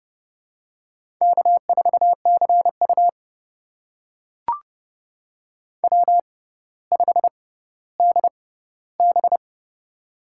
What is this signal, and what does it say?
Morse code, heard as a keyed tone, about 32 wpm, spaced as for 16 wpm, 700 Hz. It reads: K4CU W5DB
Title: USA Call Signs - 6x Character Spacing - 30wpm